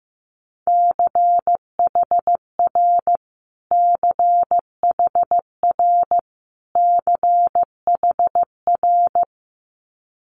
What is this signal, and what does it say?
Morse code, heard as a keyed tone, about 15 wpm, 700 Hz. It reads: CHR CHR CHR
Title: Multi-letter Phonemes Repeated 3x 15wpm